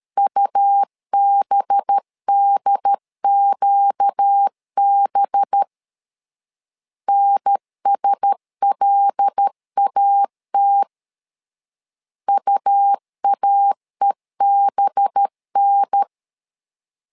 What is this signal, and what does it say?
Morse code, heard as a keyed tone, about 13 wpm, 800 Hz. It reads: UBDQB NSLAT UAEBN
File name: LEZIONI / 5 LBVC / 5 LB 12wpm 1906